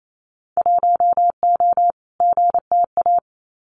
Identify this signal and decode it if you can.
Morse code, heard as a keyed tone, about 28 wpm, 700 Hz. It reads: 1O GTA
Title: Winter Field Day Exchanges 28wpm